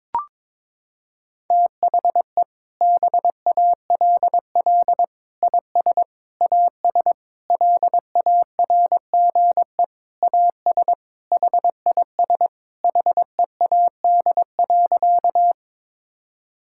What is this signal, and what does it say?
Morse code, heard as a keyed tone, about 21 wpm, 700 Hz. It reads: THE BALL IS AS LARGE AS HIS HEAD.
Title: Sentences from Top 1000 Words - Review Part 6 - 22wpm